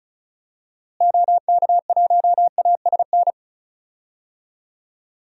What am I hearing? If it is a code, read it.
Morse code, heard as a keyed tone, about 36 wpm, 700 Hz. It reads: OK1ASN
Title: DXCC-100 Call Signs - 6 Characters - Repeated 3x - 35wpm